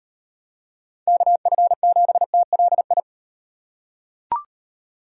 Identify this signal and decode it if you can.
Morse code, heard as a keyed tone, about 38 wpm, 700 Hz. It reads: KF7TLI